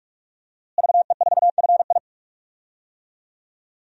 Morse code, heard as a keyed tone, about 46 wpm, 700 Hz. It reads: VE4FI
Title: DXCC-100 Call Signs - 5 Characters - Repeated 3x - 45wpm